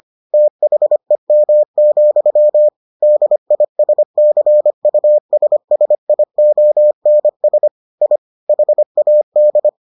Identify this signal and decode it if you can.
Morse code, heard as a keyed tone, about 25 wpm, 600 Hz. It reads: THEM, DISCUSSIONS I HAD